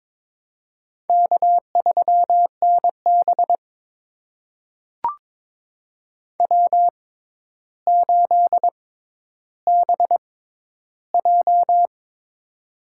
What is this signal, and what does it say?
Morse code, heard as a keyed tone, about 22 wpm, spaced as for 11 wpm, 700 Hz. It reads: K3NB W8BJ
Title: USA Call Signs - 4 Characters - 6x Character Spacing - 22wpm